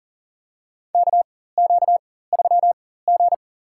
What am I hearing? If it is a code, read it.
Morse code, heard as a keyed tone, about 40 wpm, spaced as for 19 wpm, 700 Hz. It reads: KQ3G
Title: USA Call Signs - 4 Characters - 4x Character Spacing - 40wpm